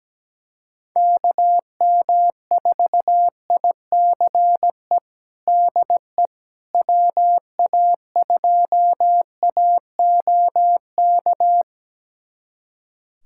Morse code, heard as a keyed tone, about 17 wpm, 700 Hz. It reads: KM4ICE DE WA2AOK
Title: USA Callsign DE Callsign - 6 Characters - 17wpm